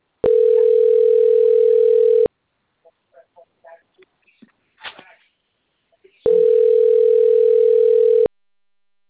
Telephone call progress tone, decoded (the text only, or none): ringback tone